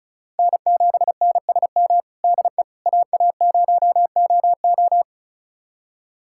Morse code, heard as a keyed tone, about 36 wpm, 700 Hz. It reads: N7NSM DE AA0OO
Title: USA Callsign DE Callsign - 5 Characters - 35wpm